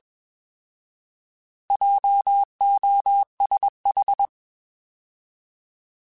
Morse code, heard as a keyed tone, about 21 wpm, 800 Hz. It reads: JOSH
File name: Alphabetical / Alphabetical mp3s / Js @20